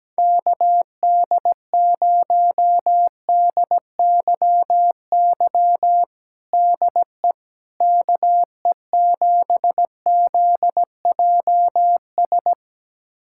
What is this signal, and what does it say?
Morse code, heard as a keyed tone, about 17 wpm, 700 Hz. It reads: KD0DYY DE KE7ZJS
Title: USA Callsign DE Callsign - 6 Characters - 17wpm